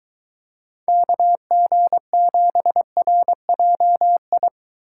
Morse code, heard as a keyed tone, about 23 wpm, 700 Hz. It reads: KG7RJI